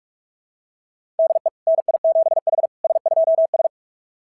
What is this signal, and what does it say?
Morse code, heard as a keyed tone, about 39 wpm, 660 Hz. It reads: DE NI7H S2S
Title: SOTA QSO Practice 45wpm